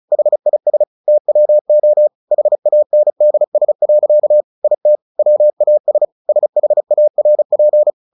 Morse code, heard as a keyed tone, about 34 wpm, 600 Hz. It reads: HIS TWO HANDS. IT WAS SHARP
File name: the old man and the sea 2 35wpm